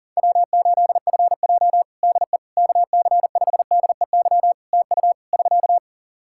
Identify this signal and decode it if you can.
Morse code, heard as a keyed tone, about 41 wpm, 700 Hz. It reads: W8FJ DE KC5DEY TU <SK>
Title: Practice QSO - Exchange 4 - Ending - 40wpm